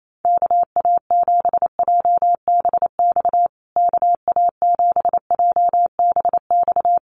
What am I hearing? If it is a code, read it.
Morse code, heard as a keyed tone, about 28 wpm, 700 Hz. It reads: KA7JBX KA7JBX
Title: USA Call Signs Repeated 3x 28wpm